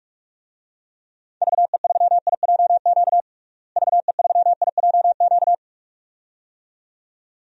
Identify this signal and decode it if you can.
Morse code, heard as a keyed tone, about 46 wpm, 700 Hz. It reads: VE3IJQ VE3IJQ
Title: DXCC-100 Call Signs - 6 Characters - Repeated 2x - 45wpm